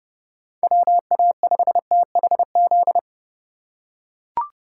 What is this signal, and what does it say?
Morse code, heard as a keyed tone, about 30 wpm, 700 Hz. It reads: WA5THZ